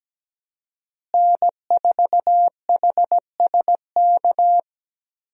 Morse code, heard as a keyed tone, about 17 wpm, 700 Hz. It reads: N4HSK
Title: USA Call Signs - 5 Characters - Repeated 3x - 17wpm